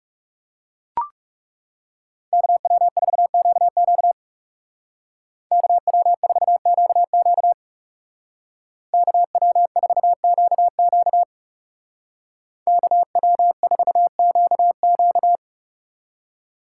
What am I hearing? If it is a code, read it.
Morse code, heard as a keyed tone, about 36 wpm, 700 Hz. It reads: KW4QQ KW4QQ KW4QQ KW4QQ
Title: Speed Racing - USA Call Signs 30wpm